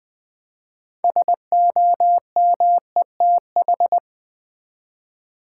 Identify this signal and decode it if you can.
Morse code, heard as a keyed tone, about 20 wpm, 700 Hz. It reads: SOMETH
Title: Build A Word - 3 Letters At A Time - 9-Letter Words - 20wpm